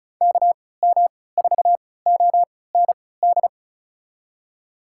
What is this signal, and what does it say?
Morse code, heard as a keyed tone, about 36 wpm, spaced as for 20 wpm, 700 Hz. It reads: KM4OND
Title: USA Call Signs - 6 Characters - 3x Character Spacing - 35wpm